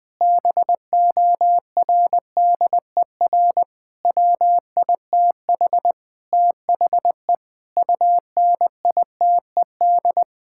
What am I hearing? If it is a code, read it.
Morse code, heard as a keyed tone, about 20 wpm, 700 Hz. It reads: BORDER WITH THE UNITED
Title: Sentences with 4-Word Phrases - Spoken Beginning - 20wpm